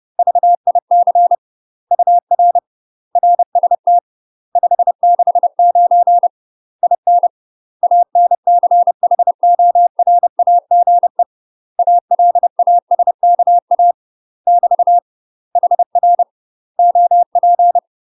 Morse code, heard as a keyed tone, about 30 wpm, 700 Hz. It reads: VIC UR RST 569 IN ANCHORAGE ALASKA <BT> HR OP
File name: UK MASTER QSOs TRAINING/30WPM